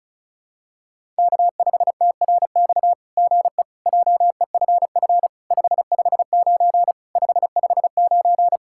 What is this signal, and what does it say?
Morse code, heard as a keyed tone, about 33 wpm, 700 Hz. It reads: K5TRX GE JEFF 559 559
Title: SOTA QSO Practice 35wpm